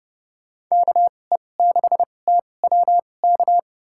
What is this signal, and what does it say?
Morse code, heard as a keyed tone, about 30 wpm, spaced as for 22 wpm, 700 Hz. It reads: KE6TWK
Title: USA Call Signs - 6 Characters - 2x Character Spacing - 30wpm